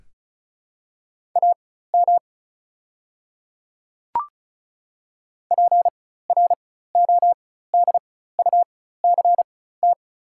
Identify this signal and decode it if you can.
Morse code, heard as a keyed tone, about 36 wpm, spaced as for 16 wpm, 690 Hz. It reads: AM PRODUCT
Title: 4x Character Spacing - Top 500 Words - 35wpm